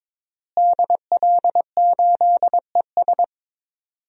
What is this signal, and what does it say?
Morse code, heard as a keyed tone, about 22 wpm, 700 Hz. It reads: DL8ES
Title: DXCC-100 Call Signs - 5 Characters - Repeated 3x - 22wpm